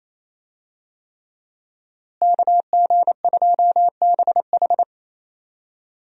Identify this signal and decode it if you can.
Morse code, heard as a keyed tone, about 27 wpm, 700 Hz. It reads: KG2BH
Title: USA Call Signs - 5 Characters - Repeated 3x - 28wpm